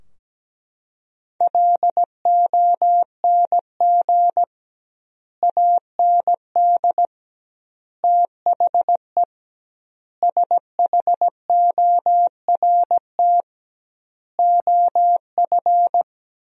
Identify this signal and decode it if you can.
Morse code, heard as a keyed tone, about 17 wpm, 700 Hz. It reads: LONG AND THE SHORT OF
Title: 2x Word-Spacing - Sentences from Top 400 Words - 17wpm